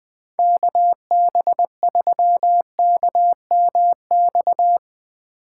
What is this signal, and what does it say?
Morse code, heard as a keyed tone, about 20 wpm, 700 Hz. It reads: KB3KMX